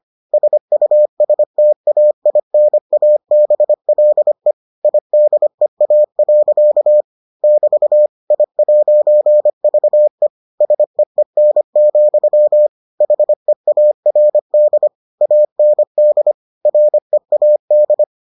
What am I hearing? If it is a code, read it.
Morse code, heard as a keyed tone, about 25 wpm, 600 Hz. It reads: SUSTAINABLE IDEA. <BT> I'VE SEEN, HEARD AND READ